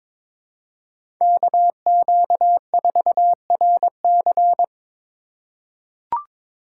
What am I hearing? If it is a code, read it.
Morse code, heard as a keyed tone, about 22 wpm, 700 Hz. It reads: KQ4RC
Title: USA Call Signs - 5 Characters - 3x Character Spacing - 22wpm